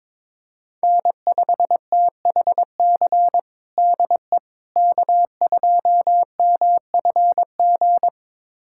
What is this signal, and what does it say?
Morse code, heard as a keyed tone, about 22 wpm, 700 Hz. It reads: N5THC DE K2MFG
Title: USA Callsign DE Callsign - 5 Characters - 22wpm